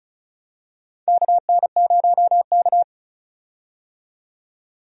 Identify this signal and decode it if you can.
Morse code, heard as a keyed tone, about 34 wpm, 700 Hz. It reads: KN0K
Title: USA Call Signs - 4 Characters - 35wpm